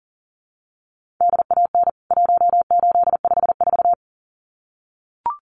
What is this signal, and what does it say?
Morse code, heard as a keyed tone, about 40 wpm, 700 Hz. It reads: DAN 1854
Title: CWT Exchange 40wpm